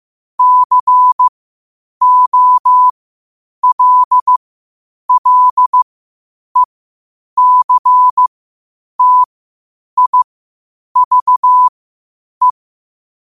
Morse code, heard as a keyed tone, about 15 wpm, spaced as for 8 wpm, 1000 Hz. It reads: COLLECTIVE
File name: qotdfiles / 8wpm